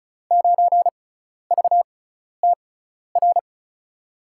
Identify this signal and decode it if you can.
Morse code, heard as a keyed tone, about 34 wpm, spaced as for 12 wpm, 700 Hz. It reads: 9VTR
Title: USA Call Signs - 5 Characters - 6x Character Spacing - 35wpm